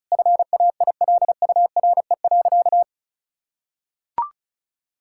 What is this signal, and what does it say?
Morse code, heard as a keyed tone, about 36 wpm, 700 Hz. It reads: FAILURE.
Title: Sentences from Top 1000 Words - Review Part 6 - 35wpm